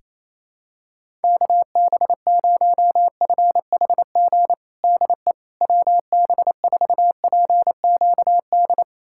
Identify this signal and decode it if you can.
Morse code, heard as a keyed tone, about 28 wpm, 700 Hz. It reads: KB0FHG DE WB4PQD
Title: USA Callsign DE Callsign - 6 Characters - 28wpm